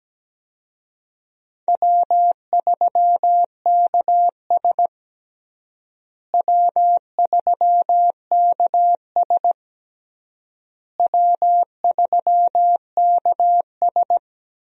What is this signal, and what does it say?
Morse code, heard as a keyed tone, about 17 wpm, 700 Hz. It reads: W3KS W3KS W3KS
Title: USA Call Signs - 4 Characters - Repeated 3x - 17wpm